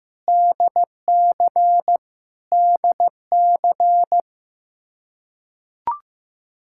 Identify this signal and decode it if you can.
Morse code, heard as a keyed tone, about 15 wpm, 700 Hz. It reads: DC DC
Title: Practice QSO - Exchange 1 - Beginning - 15wpm